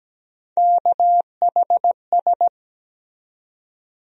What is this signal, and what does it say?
Morse code, heard as a keyed tone, about 17 wpm, 700 Hz. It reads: KHS
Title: USA Call Sign Postfixes - 3 Letters 17wpm